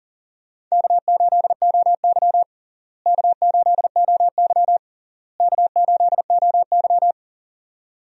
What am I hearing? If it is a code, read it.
Morse code, heard as a keyed tone, about 40 wpm, 700 Hz. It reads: K8OY K8OY K8OY
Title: USA Call Signs - 4 Characters - Repeated 3x - 40wpm